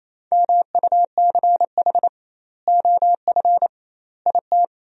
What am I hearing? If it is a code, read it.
Morse code, heard as a keyed tone, about 28 wpm, 700 Hz. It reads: MUCH OF IT
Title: 2x Word-Spacing - Sentences from Top 400 Words - 28wpm